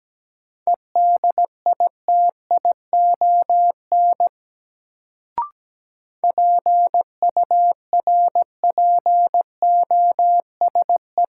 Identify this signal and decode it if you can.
Morse code, heard as a keyed tone, about 17 wpm, 700 Hz. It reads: EDITION PURPOSE
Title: Sets of 5 Words - New Word Drill - 17wpm